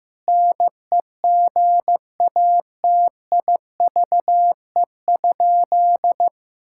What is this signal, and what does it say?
Morse code, heard as a keyed tone, about 15 wpm, 700 Hz. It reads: NEGATIVE?
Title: Sentences from Top 1000 Words - Review Part 6 - 15wpm